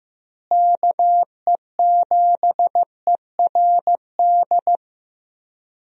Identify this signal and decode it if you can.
Morse code, heard as a keyed tone, about 15 wpm, 700 Hz. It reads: KE7ERD
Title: USA Call Signs - 6 Characters - Repeated 2x - 15wpm